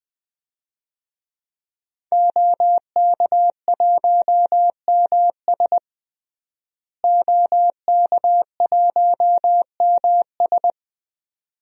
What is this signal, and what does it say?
Morse code, heard as a keyed tone, about 21 wpm, 700 Hz. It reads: OK1MS OK1MS
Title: DXCC-100 Call Signs - 5 Characters - Repeated 2x - 20wpm